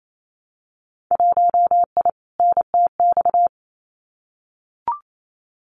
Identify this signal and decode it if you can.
Morse code, heard as a keyed tone, about 28 wpm, 700 Hz. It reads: 1I NTX
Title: Winter Field Day Exchanges 28wpm